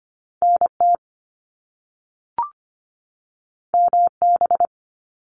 Morse code, heard as a keyed tone, about 25 wpm, 700 Hz. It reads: NT MB
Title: Canadian Provinces and Territories 25wpm